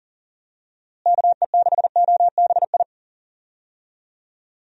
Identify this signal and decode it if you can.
Morse code, heard as a keyed tone, about 39 wpm, 700 Hz. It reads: KE6OBI
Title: USA Call Signs - 6 Characters - Repeated 3x - 40wpm